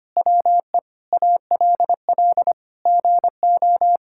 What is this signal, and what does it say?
Morse code, heard as a keyed tone, about 25 wpm, 700 Hz. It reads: WE ALL GO
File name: practice / 25wpm / Sentences with 4-Word Phrases - 25wpm